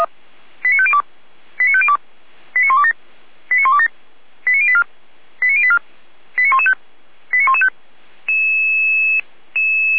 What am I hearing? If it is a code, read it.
Five-tone selective call (ZVEI), heard as quick runs of five tones, tones of about 80 ms. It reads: 89502 89502 89127 89127 89084 89084 89115 89115